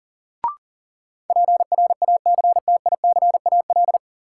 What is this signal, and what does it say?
Morse code, heard as a keyed tone, about 40 wpm, 700 Hz. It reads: PRACTICAL
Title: Sentences from Top 1400 Words - New Word Drill Alt - 40wpm